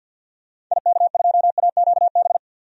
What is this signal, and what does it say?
Morse code, heard as a keyed tone, about 49 wpm, 700 Hz. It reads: IK2AQB
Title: DXCC-100 Call Signs - 6 Characters - 50wpm